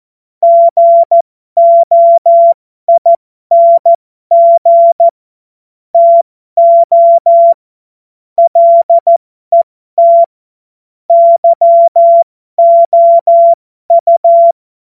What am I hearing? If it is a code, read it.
Morse code, heard as a keyed tone, about 14 wpm, spaced as for 12 wpm, 680 Hz. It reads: GOING TO LET YOU
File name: Morsekurs / Witze